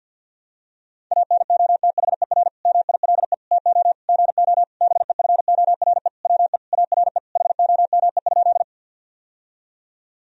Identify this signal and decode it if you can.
Morse code, heard as a keyed tone, about 50 wpm, 700 Hz. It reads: ANOTHER MILE TO GO BEFORE WE ARE HOME?
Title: Sentences from Top 1000 Words - Review Part 1 - 50wpm